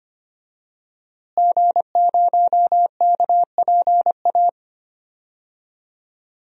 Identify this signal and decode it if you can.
Morse code, heard as a keyed tone, about 25 wpm, 700 Hz. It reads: G0KPA
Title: DXCC-100 Call Signs - 5 Characters - Repeated 2x - 25wpm